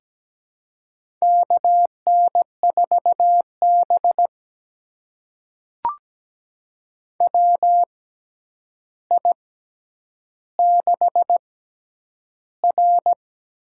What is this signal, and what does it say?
Morse code, heard as a keyed tone, about 17 wpm, spaced as for 9 wpm, 700 Hz. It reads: KN4B WI6R
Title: USA Call Signs - 4 Characters - 6x Character Spacing - 17wpm